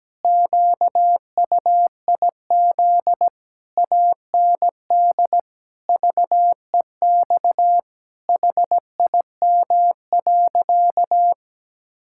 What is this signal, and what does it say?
Morse code, heard as a keyed tone, about 17 wpm, 700 Hz. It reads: QUIZ AND VEX HIM.
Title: Pangram Challenge v2 - 17wpm